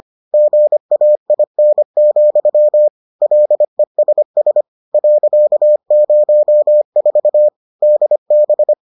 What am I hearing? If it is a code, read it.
Morse code, heard as a keyed tone, about 25 wpm, 600 Hz. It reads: GAIN, LESS .04 DB